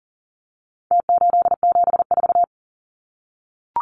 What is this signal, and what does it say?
Morse code, heard as a keyed tone, about 40 wpm, 700 Hz. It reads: T874
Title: Numbers - Contest Style - 4-Digit 40wpm